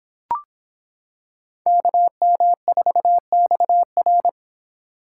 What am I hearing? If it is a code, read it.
Morse code, heard as a keyed tone, about 26 wpm, 700 Hz. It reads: KM4XR